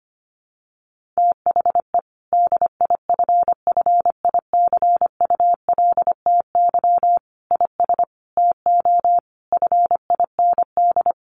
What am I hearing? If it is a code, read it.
Morse code, heard as a keyed tone, about 25 wpm, 700 Hz. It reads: THE DIFFICULTY IS TO FIND
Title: Sentences from Top 1500 Words 25wpm